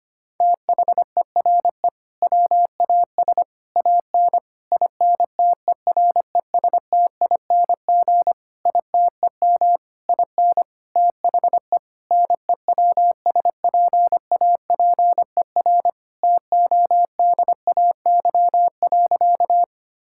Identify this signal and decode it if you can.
Morse code, heard as a keyed tone, about 25 wpm, 700 Hz. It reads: THERE WAS AN INTERESTING ITEM IN THE NEWSPAPER TODAY.
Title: Sentences from Top 1000 Words - Review Part 6 - 25wpm